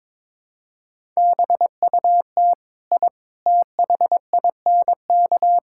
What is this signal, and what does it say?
Morse code, heard as a keyed tone, about 22 wpm, 700 Hz. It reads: BUT I THINK
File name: practice / 22wpm / Sentences with 3-Word Phrases - 22wpm